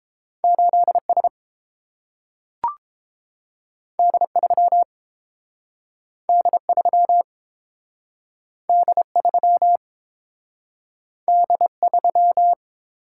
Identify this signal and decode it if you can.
Morse code, heard as a keyed tone, about 30 wpm, 700 Hz. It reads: 8S D3 D3 D3 D3